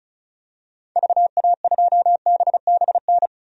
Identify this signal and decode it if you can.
Morse code, heard as a keyed tone, about 34 wpm, 700 Hz. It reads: VA2BBN
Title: DXCC-100 Call Signs - 6 Characters - 35wpm